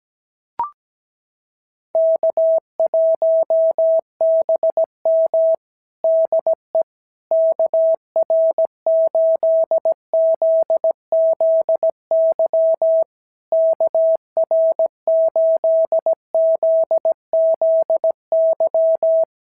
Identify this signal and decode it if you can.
Morse code, heard as a keyed tone, about 18 wpm, 650 Hz. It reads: K1BM DE KR8ZZY KR8ZZY
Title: SOTA QSO Practice 17wpm